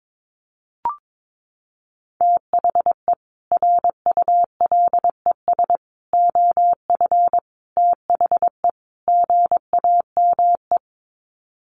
Morse code, heard as a keyed tone, about 22 wpm, 700 Hz. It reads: THE RULES OF THE GAME
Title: Sets of 5 Words Encore2 22wpm